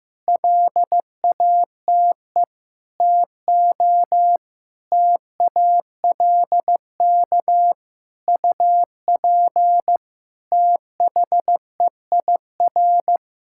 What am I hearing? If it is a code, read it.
Morse code, heard as a keyed tone, about 15 wpm, 700 Hz. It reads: LATE TO TALK UP THEIR
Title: Sentences from Top 1600 Words 15wpm